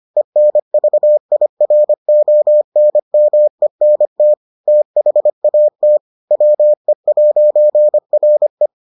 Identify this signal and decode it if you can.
Morse code, heard as a keyed tone, about 25 wpm, 600 Hz. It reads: ENVIRONMENT THAT WE'RE